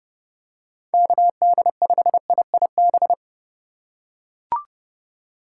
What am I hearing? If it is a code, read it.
Morse code, heard as a keyed tone, about 28 wpm, 700 Hz. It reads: KD5IIB